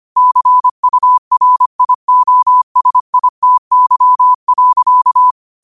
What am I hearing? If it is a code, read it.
Morse code, heard as a keyed tone, about 25 wpm, 1000 Hz. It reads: CURIOSITY.
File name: qotdfiles / 25wpm